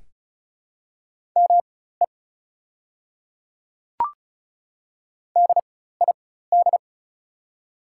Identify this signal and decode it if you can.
Morse code, heard as a keyed tone, about 36 wpm, spaced as for 16 wpm, 700 Hz. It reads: ME DID
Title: 4x Character Spacing - Top 200 Words - 35wpm